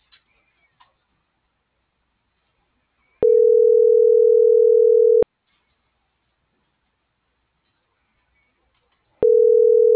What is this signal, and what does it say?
Ringback tone (North America), 440 + 480 Hz.